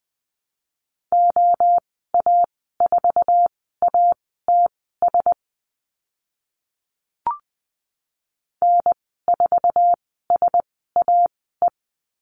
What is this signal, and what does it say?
Morse code, heard as a keyed tone, about 20 wpm, spaced as for 14 wpm, 700 Hz. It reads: OA4ATS N4SAE
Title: 055 Lesson 10g - Letter r - Call Signs 20wpm